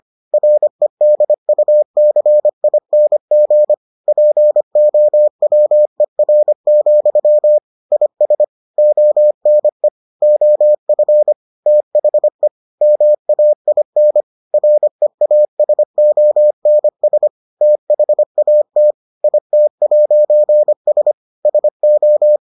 Morse code, heard as a keyed tone, about 25 wpm, 600 Hz. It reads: REDUCING POWER, IS ONE OF THE MAIN REASONS THAT IT'S SO